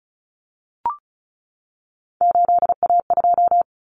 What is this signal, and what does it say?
Morse code, heard as a keyed tone, about 34 wpm, 700 Hz. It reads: 8A2
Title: Common Cut Numbers - 3-Digit 35wpm